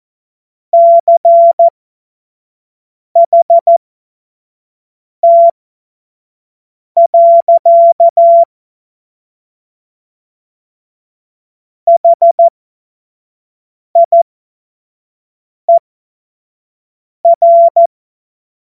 Morse code, heard as a keyed tone, about 14 wpm, spaced as for 5 wpm, 680 Hz. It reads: CHT. HIER
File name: Witz-16-September-2025-25-F70